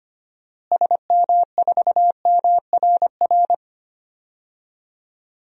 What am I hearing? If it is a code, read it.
Morse code, heard as a keyed tone, about 25 wpm, 700 Hz. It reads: SM4MRR
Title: DXCC-100 Call Signs - 6 Characters - Repeated 2x - 25wpm